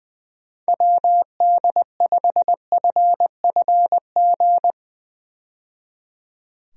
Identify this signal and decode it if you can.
Morse code, heard as a keyed tone, about 21 wpm, 700 Hz. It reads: WD5FFG
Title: USA Call Signs - 6 Characters - 20wpm